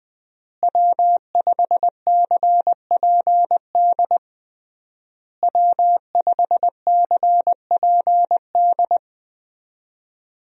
Morse code, heard as a keyed tone, about 21 wpm, 700 Hz. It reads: W5CPD W5CPD
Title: USA Call Signs - 5 Characters - Repeated 3x - 20wpm